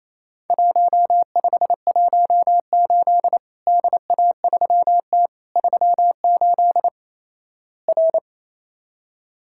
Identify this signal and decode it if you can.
Morse code, heard as a keyed tone, about 28 wpm, 700 Hz. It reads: 1518 DA3T 38 R
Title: Worked All Europe DX Contest - QTC Exchange - 28wpm